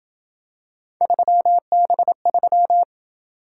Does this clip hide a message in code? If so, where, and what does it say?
Yes: Morse code from 1.0 to 2.8 s, heard as a keyed tone, about 27 wpm, 700 Hz. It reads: 3B3